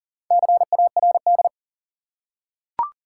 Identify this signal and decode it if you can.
Morse code, heard as a keyed tone, about 37 wpm, 700 Hz. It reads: CARD
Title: Four Letter Words - 40wpm